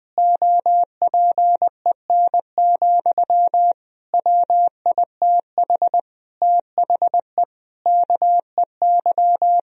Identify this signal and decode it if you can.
Morse code, heard as a keyed tone, about 20 wpm, 700 Hz. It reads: OPEN, WITH THE KEY
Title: Sentences from Top 1000 Words - Review Part 4 - 20wpm